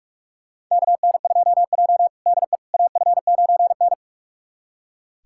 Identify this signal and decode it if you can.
Morse code, heard as a keyed tone, about 46 wpm, 700 Hz. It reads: KN2J DE AF9N
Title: USA Callsign DE Callsign - 4 Characters - 45wpm